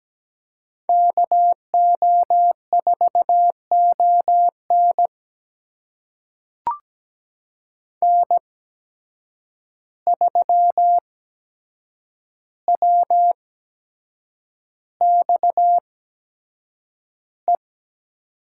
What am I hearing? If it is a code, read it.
Morse code, heard as a keyed tone, about 17 wpm, spaced as for 7 wpm, 700 Hz. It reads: KO4ON N3WXE
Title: USA Call Signs - 8x Character Spacing - 17wpm